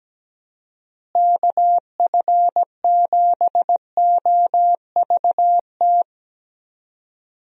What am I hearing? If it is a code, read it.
Morse code, heard as a keyed tone, about 17 wpm, 700 Hz. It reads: KF7OVT
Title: USA Call Signs - 6 Characters - Repeated 2x - 17wpm